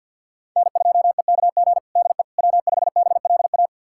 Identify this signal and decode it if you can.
Morse code, heard as a keyed tone, about 52 wpm, 700 Hz. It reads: N2EKG DE W5BLA
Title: USA Callsign DE Callsign - 5 Characters - 50wpm